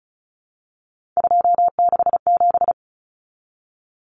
Morse code, heard as a keyed tone, about 34 wpm, 700 Hz. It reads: 267
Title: Common Cut Numbers - 3-Digit 35wpm